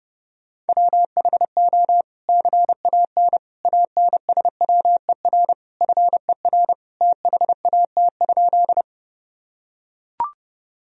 Morse code, heard as a keyed tone, about 30 wpm, 700 Hz. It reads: WHO CAN ANSWER FER THAT?
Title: Sentences from Top 300 Words and Abbreviations 30wpm